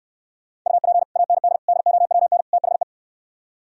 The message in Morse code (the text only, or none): UP AND DOWN THE